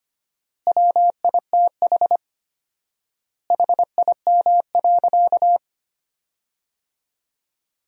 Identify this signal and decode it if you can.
Morse code, heard as a keyed tone, about 25 wpm, 700 Hz. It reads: WITH HIM.
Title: 4x Word-Spacing - Sentences from Top 400 Words - 25wpm